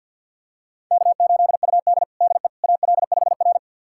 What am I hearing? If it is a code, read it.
Morse code, heard as a keyed tone, about 52 wpm, 700 Hz. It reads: K8UD DE AL5R
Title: USA Callsign DE Callsign - 4 Characters - 50wpm